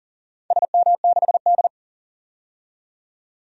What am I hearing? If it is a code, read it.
Morse code, heard as a keyed tone, about 40 wpm, 700 Hz. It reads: SM6D
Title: DXCC-100 Call Signs - 4 Characters - 40wpm